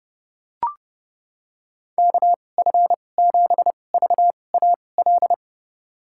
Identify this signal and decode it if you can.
Morse code, heard as a keyed tone, about 30 wpm, spaced as for 22 wpm, 700 Hz. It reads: KF7VAL
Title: USA Call Signs - 6 Characters - 2x Character Spacing - 30wpm